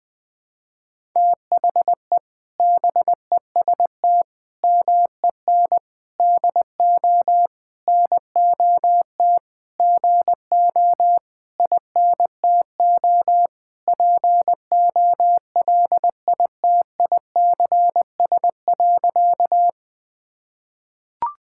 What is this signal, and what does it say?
Morse code, heard as a keyed tone, about 20 wpm, 700 Hz. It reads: THE BEST MEN DO NOT GO INTO POLITICS.
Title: Sentences from Top 1000 Words - Review Part 6 - 20wpm